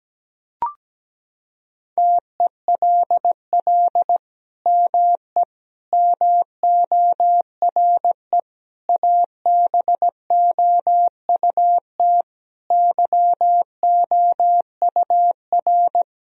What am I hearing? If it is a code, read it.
Morse code, heard as a keyed tone, about 17 wpm, 700 Hz. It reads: TELL ME MORE ABOUT YOUR
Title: Sentences from Top 1000 Words - Review Part 4 - 17wpm